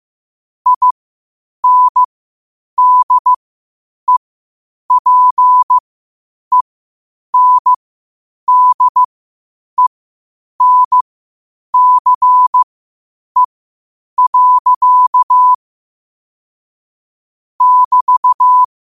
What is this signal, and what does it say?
Morse code, heard as a keyed tone, about 15 wpm, spaced as for 8 wpm, 1000 Hz. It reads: INDEPENDENCE. <BT>
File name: qotdfiles / 8wpm